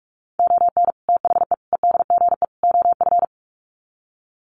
Morse code, heard as a keyed tone, about 44 wpm, 700 Hz. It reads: ON THE EDGE OF
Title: Sets of 4 Words 45wpm